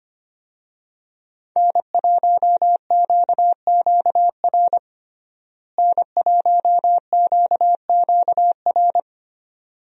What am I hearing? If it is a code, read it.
Morse code, heard as a keyed tone, about 25 wpm, 700 Hz. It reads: N1QQR N1QQR
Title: USA Call Signs - 5 Characters - Repeated 2x - 25wpm